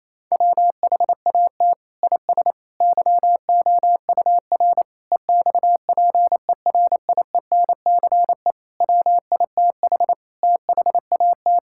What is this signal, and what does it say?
Morse code, heard as a keyed tone, about 28 wpm, 700 Hz. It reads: WHAT IS YOUR EXPERIENCE WITH THAT
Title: Sentences from Top 1000 Words - Review Part 5 - 28wpm